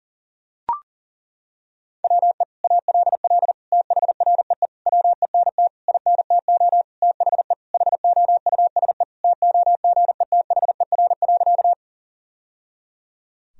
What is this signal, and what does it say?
Morse code, heard as a keyed tone, about 40 wpm, 700 Hz. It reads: WE ALL THREE WENT INTO THE HOUSE TOGETHER.
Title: Sentences from Top 1000 Words - Review Part 1 - 40wpm